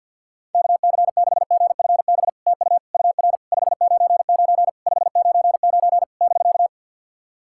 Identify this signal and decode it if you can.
Morse code, heard as a keyed tone, about 50 wpm, 700 Hz. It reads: KC6GFB TU UR 599 599 <BK>
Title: SOTA QSO Practice 50wpm